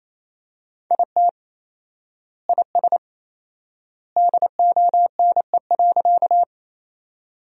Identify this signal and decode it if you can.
Morse code, heard as a keyed tone, about 28 wpm, 700 Hz. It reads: IT IS DONE.
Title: 4x Word-Spacing - Sentences from Top 500 Words - 28wpm